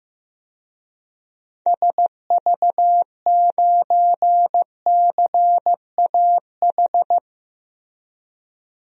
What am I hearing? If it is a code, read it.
Morse code, heard as a keyed tone, about 15 wpm, 700 Hz. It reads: SV9CAH
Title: DXCC-100 Call Signs - 6 Characters - 15wpm